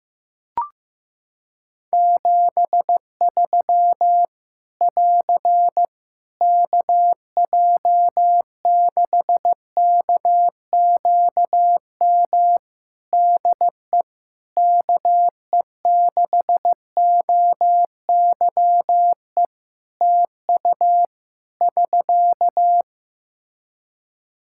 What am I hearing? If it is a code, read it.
Morse code, heard as a keyed tone, about 15 wpm, 700 Hz. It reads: 73 <AR> KJ6KQM DE KE6OYE TU <SK>
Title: Practice QSO - Exchange 4 - Ending - 15wpm